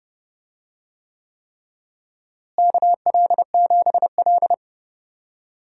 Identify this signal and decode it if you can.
Morse code, heard as a keyed tone, about 32 wpm, 700 Hz. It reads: KL7L
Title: DXCC-100 Call Signs - 4 Characters - Repeated 3x - 30wpm